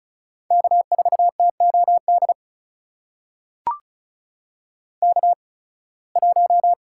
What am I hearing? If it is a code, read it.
Morse code, heard as a keyed tone, about 36 wpm, spaced as for 23 wpm, 700 Hz. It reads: K4TOD K1
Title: USA Call Signs - 5 Characters - 8x Character Spacing - 35wpm